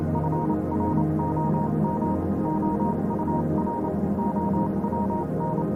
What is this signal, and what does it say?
Morse code, heard as a keyed tone, about 28 wpm, spaced as for 25 wpm, 940 Hz, with noise. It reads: FROM OUR OWN
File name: Vylet Pony / aura / do you remember the song she sang (morse code version)